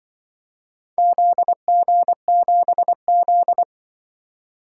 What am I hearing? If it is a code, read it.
Morse code, heard as a keyed tone, about 25 wpm, 700 Hz. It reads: ZG7Z
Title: Speed-Racing - Four Letter-Number Combinations - 20wpm